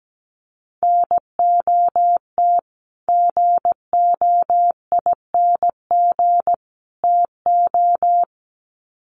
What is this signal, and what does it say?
Morse code, heard as a keyed tone, about 17 wpm, 700 Hz. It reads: NOT GOING TO
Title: Sets of 3 Words 17wpm